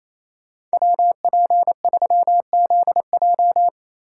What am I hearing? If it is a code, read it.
Morse code, heard as a keyed tone, about 36 wpm, 700 Hz. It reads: WP3ZJ